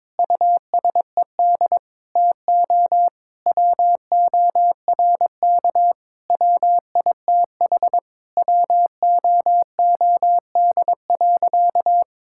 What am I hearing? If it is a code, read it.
Morse code, heard as a keyed tone, about 22 wpm, 700 Hz. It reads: USED TO WORK WITH WOOD.
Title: Sentences from Top 1000 Words - Review Part 6 - 22wpm